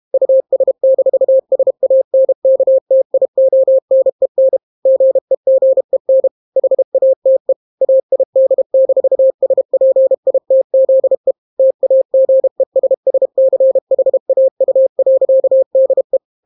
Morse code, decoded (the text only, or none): US-SANKTIONEN GEGEN HATE AID-SPITZE TAGESSCHAU.DE